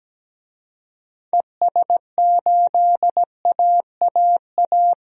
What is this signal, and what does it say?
Morse code, heard as a keyed tone, about 17 wpm, 700 Hz. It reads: ES8AAA
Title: DXCC-100 Call Signs - 6 Characters - 17wpm